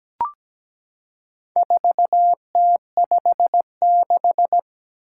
Morse code, heard as a keyed tone, about 17 wpm, 700 Hz. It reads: 4T56
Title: Four Challenging Letter-Number Combinations 17wpm